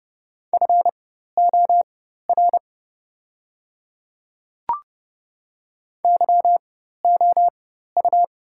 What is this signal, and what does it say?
Morse code, heard as a keyed tone, about 30 wpm, spaced as for 14 wpm, 700 Hz. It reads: FOR YOU
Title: 4x Character Spacing - Top 100 Words - 30wpm